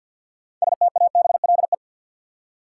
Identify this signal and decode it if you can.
Morse code, heard as a keyed tone, about 51 wpm, 700 Hz. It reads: STABLE
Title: Six Letter Words - 50wpm